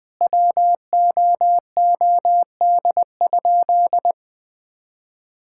Morse code, heard as a keyed tone, about 20 wpm, 700 Hz. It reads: WOOD?
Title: Sentences from Top 1600 Words 20wpm